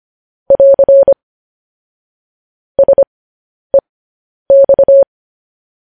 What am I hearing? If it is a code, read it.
Morse code, heard as a keyed tone, about 25 wpm, spaced as for 10 wpm, 550 Hz. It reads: <AR> SEX
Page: news headlines in morse code at 10 wpm